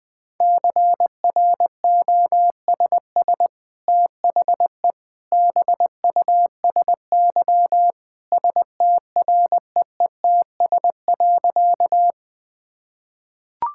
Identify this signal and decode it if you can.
Morse code, heard as a keyed tone, about 20 wpm, 700 Hz. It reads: CROSS THE BUSY STREETS.
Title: Easy Sentences with Hints - 20wpm